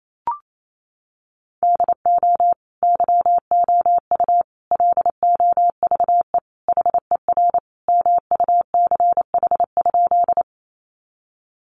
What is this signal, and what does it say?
Morse code, heard as a keyed tone, about 28 wpm, 700 Hz. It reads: DO YOU LOVE HER MUCH?
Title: Sentences from Top 400 Words 28wpm